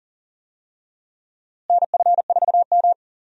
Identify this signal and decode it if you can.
Morse code, heard as a keyed tone, about 37 wpm, 700 Hz. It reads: NF4M
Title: USA Call Signs - 4 Characters - Repeated 2x - 40wpm